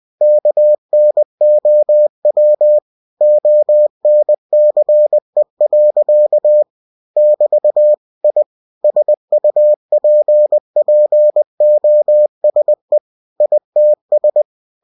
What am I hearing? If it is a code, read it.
Morse code, heard as a keyed tone, about 20 wpm, 600 Hz. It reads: KNOW ONCE. <BT> I SUPPOSE ITS
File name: cw-books / winnie-the-pooh / winnie-20wpm-page-0085